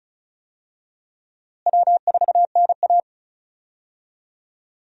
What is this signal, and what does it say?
Morse code, heard as a keyed tone, about 36 wpm, 700 Hz. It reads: W4NA
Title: USA Call Signs - 4 Characters - Repeated 2x - 35wpm